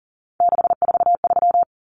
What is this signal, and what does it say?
Morse code, heard as a keyed tone, about 40 wpm, 700 Hz. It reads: 643
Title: Numbers - 3-Digit 40wpm